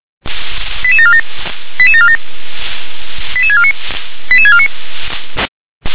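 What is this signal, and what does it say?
Five-tone selective call (ZVEI), heard as quick runs of five tones, tones of about 70 ms. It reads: 88747 88747 88649 88649
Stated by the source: ZVEI five-tone selective call received on SDR.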